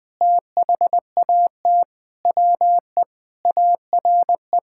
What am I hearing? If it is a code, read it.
Morse code, heard as a keyed tone, about 20 wpm, 700 Hz. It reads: THAT WE ARE
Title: Sentences with 7-Word Phrases - 20wpm